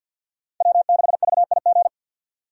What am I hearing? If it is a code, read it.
Morse code, heard as a keyed tone, about 52 wpm, 700 Hz. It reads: W6VIG
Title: USA Call Signs - 5 Characters - 50wpm